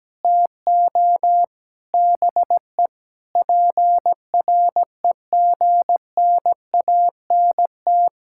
Morse code, heard as a keyed tone, about 17 wpm, 700 Hz. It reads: TO BE PREGNANT
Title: Sentences from Top 1000 Words - Review Part 6 - 17wpm